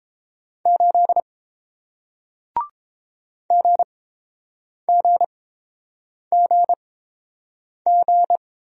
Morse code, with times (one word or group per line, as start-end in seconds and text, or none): 0.7-1.2 8
3.5-8.4 GGGG